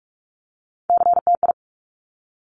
Morse code, heard as a keyed tone, about 46 wpm, 700 Hz. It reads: CTI
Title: Top Three Letter Combinations in Words 45wpm